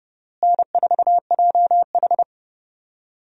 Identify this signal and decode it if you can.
Morse code, heard as a keyed tone, about 32 wpm, 700 Hz. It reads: N4JH
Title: USA Call Signs - 4 Characters - Repeated 3x - 30wpm